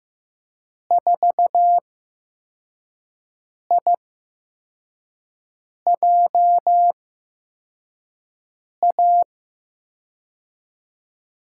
Morse code, heard as a keyed tone, about 15 wpm, spaced as for 4 wpm, 700 Hz. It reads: 4IJA
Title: USA Call Signs - 8x Character Spacing - 15wpm